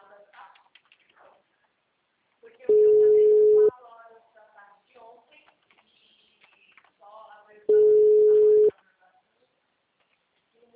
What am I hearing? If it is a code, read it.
Ringback tone (Europe, 425 Hz), with speech.